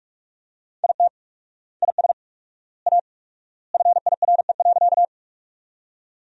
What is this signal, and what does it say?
Morse code, heard as a keyed tone, about 41 wpm, 700 Hz. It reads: IT IS A FIRE.
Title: 4x Word-Spacing - Sentences from Top 400 Words - 45wpm